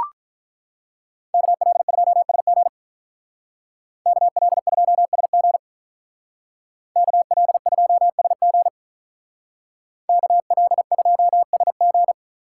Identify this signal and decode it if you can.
Morse code, heard as a keyed tone, about 43 wpm, 700 Hz. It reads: KL2SG KL2SG KL2SG KL2SG
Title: Speed Racing - USA Call Signs 35wpm